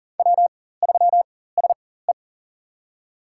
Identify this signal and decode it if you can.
Morse code, heard as a keyed tone, about 40 wpm, spaced as for 19 wpm, 700 Hz. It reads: W3SE
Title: USA Call Signs - 4x Character Spacing - 40wpm